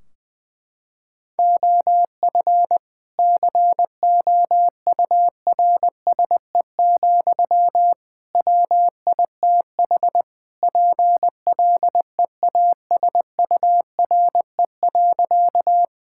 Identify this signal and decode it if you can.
Morse code, heard as a keyed tone, about 20 wpm, 700 Hz. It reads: OF COURSE, WITH PLEASURE.
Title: Sentences from Top 1600 Words 20wpm